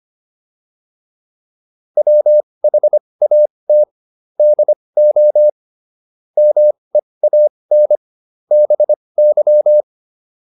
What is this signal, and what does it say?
Morse code, heard as a keyed tone, about 25 wpm, spaced as for 20 wpm, 600 Hz. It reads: WHAT DO MEAN BY